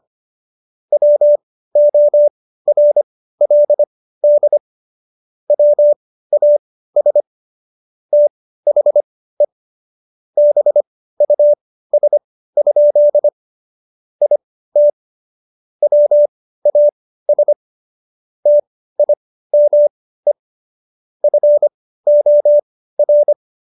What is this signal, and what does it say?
Morse code, heard as a keyed tone, about 25 wpm, spaced as for 15 wpm, 590 Hz. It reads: WORLD WAS THE BUS? IT WAS TIME FOR